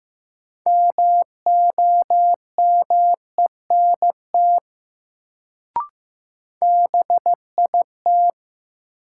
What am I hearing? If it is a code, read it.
Morse code, heard as a keyed tone, about 15 wpm, 700 Hz. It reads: MOMENT BIT
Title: Sets of 3 Words - New Word Drill - 15wpm